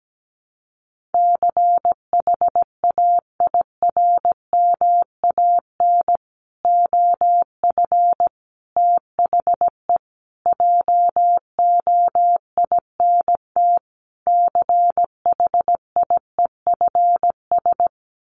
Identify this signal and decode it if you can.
Morse code, heard as a keyed tone, about 17 wpm, 700 Hz. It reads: CHAIRMAN OF THE JOINT CHIEFS
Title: Sets of 5 Words Encore 17wpm